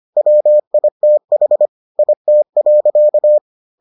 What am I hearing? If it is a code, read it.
Morse code, heard as a keyed tone, about 25 wpm, 600 Hz. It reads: WITH IT.